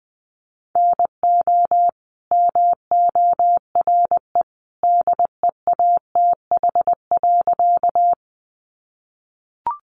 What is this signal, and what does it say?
Morse code, heard as a keyed tone, about 20 wpm, 700 Hz. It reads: NO MORE DEATH.
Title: Sentences from Top 1100 Words 20wpm